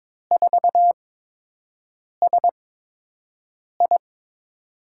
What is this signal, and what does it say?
Morse code, heard as a keyed tone, about 23 wpm, spaced as for 6 wpm, 700 Hz. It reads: 4SI
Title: USA Call Signs - 5 Characters - 8x Character Spacing - 22wpm